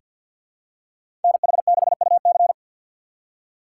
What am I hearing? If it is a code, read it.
Morse code, heard as a keyed tone, about 46 wpm, 690 Hz. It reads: NH6UC
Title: USA Call Signs - 2x Character Spacing - 50wpm